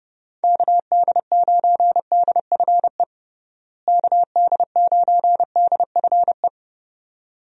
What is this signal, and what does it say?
Morse code, heard as a keyed tone, about 30 wpm, 700 Hz. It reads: KD9DFE KD9DFE
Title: USA Call Signs - 6 Characters - Repeated 2x - 30wpm